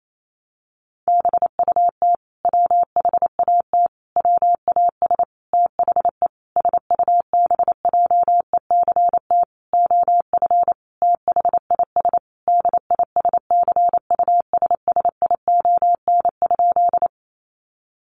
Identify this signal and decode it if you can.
Morse code, heard as a keyed tone, about 28 wpm, 700 Hz. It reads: BUT WHAT WAS THE SUBJECT OF THIS DISCUSSION?
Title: Sentences from Top 1100 Words 28wpm